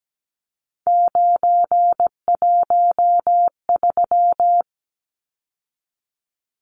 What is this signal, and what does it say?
Morse code, heard as a keyed tone, about 17 wpm, 700 Hz. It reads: 913
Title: Numbers - 3-Digit 17wpm